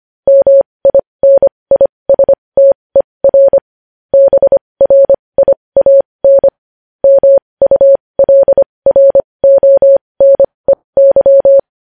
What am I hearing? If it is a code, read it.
Morse code, heard as a keyed tone, about 25 wpm, spaced as for 20 wpm, 550 Hz. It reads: MINISTER BRIAN MULRONEY
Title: News Headlines at 20